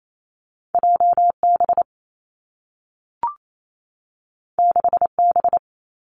Morse code, heard as a keyed tone, about 28 wpm, 700 Hz. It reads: J B 6 B
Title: Two Challenging Letter-Number Combinations 28wpm